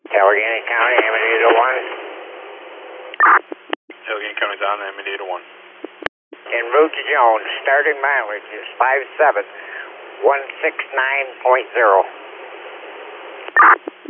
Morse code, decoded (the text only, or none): none